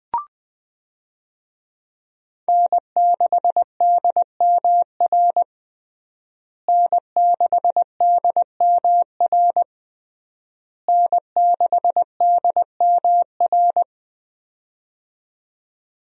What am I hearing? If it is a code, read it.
Morse code, heard as a keyed tone, about 20 wpm, 700 Hz. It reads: N6DMR N6DMR N6DMR
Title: USA Call Signs - 5 Characters - Repeated 3x - 20wpm